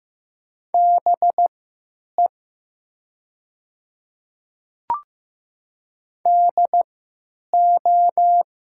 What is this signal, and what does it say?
Morse code, heard as a keyed tone, about 15 wpm, spaced as for 9 wpm, 700 Hz. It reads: BE DO
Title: 3x Character Spacing - Top 200 Words - 15wpm